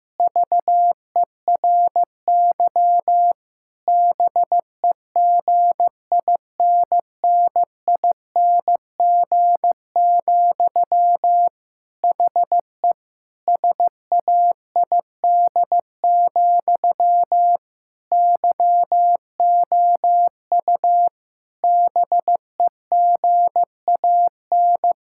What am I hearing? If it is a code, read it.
Morse code, heard as a keyed tone, about 16 wpm, 700 Hz. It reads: VERY BEGINNING, HE SAID, YOU BEGAN
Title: Sentences from Top 1600 Words 15wpm